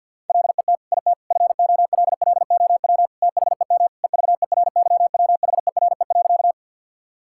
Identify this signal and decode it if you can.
Morse code, heard as a keyed tone, about 50 wpm, 700 Hz. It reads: LET IT FOLLOW THEM EVERYWHERE.
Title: Sentences from Top 1600 Words 50wpm